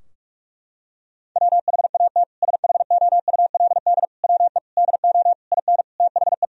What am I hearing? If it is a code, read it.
Morse code, heard as a keyed tone, about 41 wpm, 700 Hz. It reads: WHAT SHOULD WE DO IN THE
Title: Sentences from Top 1000 Words - Review Part 4 - 45wpm